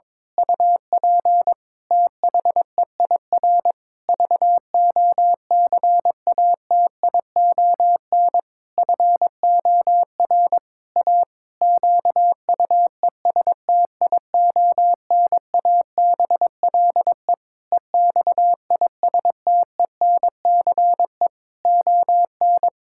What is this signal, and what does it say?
Morse code, heard as a keyed tone, about 22 wpm, 700 Hz. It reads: UP THEIR VOCATION FOR A QUESTIONABLE EXISTENCE ON
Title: Pangram Challenge v2 - 22wpm